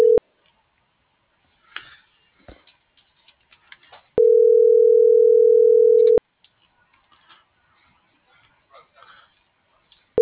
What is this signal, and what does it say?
Ringback tone (North America), 440 + 480 Hz.